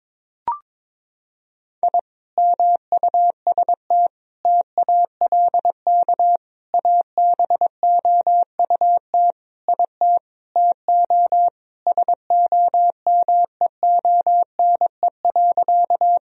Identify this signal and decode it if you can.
Morse code, heard as a keyed tone, about 22 wpm, 700 Hz. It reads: I MUST TALK ABOUT IT TO SOMEONE.
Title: Sentences from Top 1000 Words - Review Part 4 - 22wpm